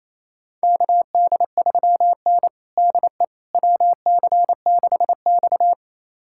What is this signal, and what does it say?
Morse code, heard as a keyed tone, about 28 wpm, 700 Hz. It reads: KD3N DE WC6X
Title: USA Callsign DE Callsign - 4 Characters - 28wpm